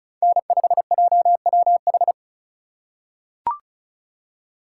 Morse code, heard as a keyed tone, about 34 wpm, 700 Hz. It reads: N5JWH
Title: USA Call Signs - 5 Characters - 3x Character Spacing - 35wpm